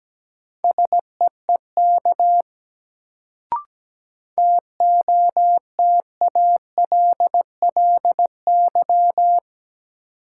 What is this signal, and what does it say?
Morse code, heard as a keyed tone, about 17 wpm, 700 Hz. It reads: SEEK TOTALLY
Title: Sentences from Top 1600 Words - New Word Drill - 17wpm